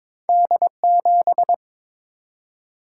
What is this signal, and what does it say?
Morse code, heard as a keyed tone, about 22 wpm, 700 Hz. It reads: D7
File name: practice / 22wpm / Two Letter-Number Combinations - 22wpm